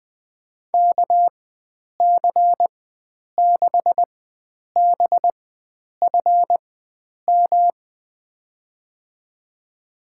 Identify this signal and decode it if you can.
Morse code, heard as a keyed tone, about 21 wpm, spaced as for 9 wpm, 700 Hz. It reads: KC6BFM
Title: USA Call Signs - 6 Characters - 4x Character Spacing - 20wpm